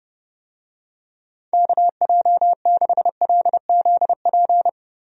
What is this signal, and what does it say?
Morse code, heard as a keyed tone, about 30 wpm, 700 Hz. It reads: KJ6LZP
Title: USA Call Signs - 6 Characters - Repeated 2x - 30wpm